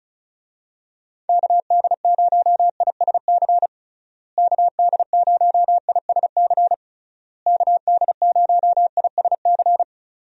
Morse code, heard as a keyed tone, about 36 wpm, 700 Hz. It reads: KD0ISC KD0ISC KD0ISC
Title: USA Call Signs - 6 Characters - Repeated 3x - 35wpm